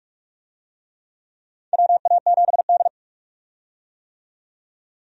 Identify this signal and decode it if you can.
Morse code, heard as a keyed tone, about 44 wpm, 700 Hz. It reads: WA7D
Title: USA Call Signs - 4 Characters - 45wpm